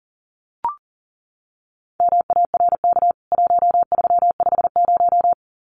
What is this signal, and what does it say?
Morse code, heard as a keyed tone, about 40 wpm, 700 Hz. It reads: MARK 1350
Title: CWT Exchange 40wpm